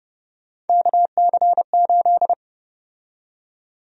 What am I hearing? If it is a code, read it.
Morse code, heard as a keyed tone, about 30 wpm, 700 Hz. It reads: KC8
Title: USA Call Sign Prefixes - 30wpm